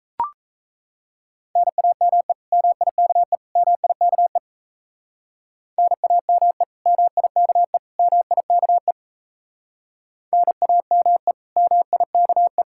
Morse code, heard as a keyed tone, about 35 wpm, 700 Hz. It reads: NAME MIKE MIKE NAME MIKE MIKE NAME MIKE